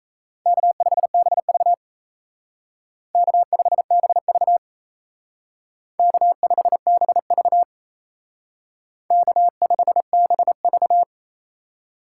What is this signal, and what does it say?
Morse code, heard as a keyed tone, about 34 wpm, 700 Hz. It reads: K5BV K5BV K5BV K5BV
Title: Speed Racing - USA Call Signs 28wpm